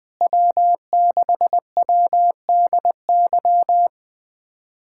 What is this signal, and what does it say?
Morse code, heard as a keyed tone, about 21 wpm, 700 Hz. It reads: W6WDY
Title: USA Callsign DE Callsign - 5 Characters - 20wpm